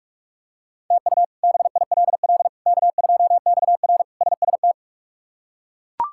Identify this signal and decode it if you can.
Morse code, heard as a keyed tone, about 42 wpm, 700 Hz. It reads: TU BILL K2XR SST
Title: K1USN SST QSO Practice - 45wpm